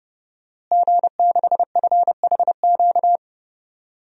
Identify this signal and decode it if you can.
Morse code, heard as a keyed tone, about 32 wpm, 700 Hz. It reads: G6FHQ
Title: DXCC-100 Call Signs - 5 Characters - Repeated 3x - 30wpm